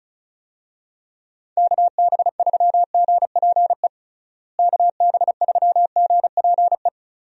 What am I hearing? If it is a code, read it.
Morse code, heard as a keyed tone, about 36 wpm, 700 Hz. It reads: KB3GPE KB3GPE
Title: USA Call Signs - 6 Characters - Repeated 2x - 35wpm